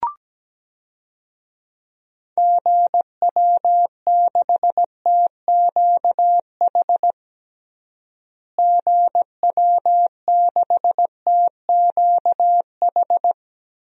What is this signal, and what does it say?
Morse code, heard as a keyed tone, about 17 wpm, 700 Hz. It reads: GW6TQH GW6TQH
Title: DXCC-100 Call Signs - 6 Characters - Repeated 2x - 17wpm